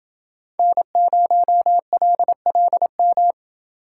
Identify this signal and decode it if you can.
Morse code, heard as a keyed tone, about 29 wpm, 700 Hz. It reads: N0LLM